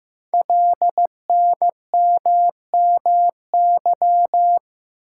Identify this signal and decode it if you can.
Morse code, heard as a keyed tone, about 15 wpm, 700 Hz. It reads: LNMMY
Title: Code Groups - 5 Letters - 15wpm